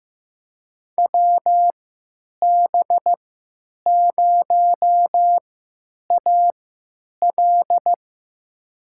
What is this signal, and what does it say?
Morse code, heard as a keyed tone, about 15 wpm, spaced as for 9 wpm, 700 Hz. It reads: WB0AL
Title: USA Call Signs - 5 Characters - 3x Character Spacing - 15wpm